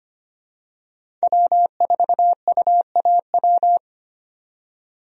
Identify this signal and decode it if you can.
Morse code, heard as a keyed tone, about 25 wpm, 700 Hz. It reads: W4UAW
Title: USA Call Signs - 5 Characters - 25wpm